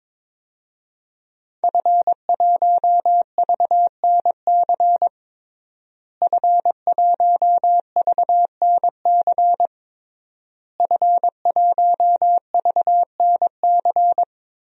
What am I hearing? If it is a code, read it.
Morse code, heard as a keyed tone, about 22 wpm, 700 Hz. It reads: F1VNC F1VNC F1VNC
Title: DXCC-100 Call Signs - 5 Characters - Repeated 3x - 22wpm